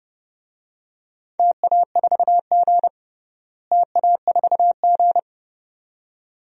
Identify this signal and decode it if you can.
Morse code, heard as a keyed tone, about 30 wpm, 700 Hz. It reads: TA4G TA4G
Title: DXCC-100 Call Signs - 4 Characters - Repeated 2x - 30wpm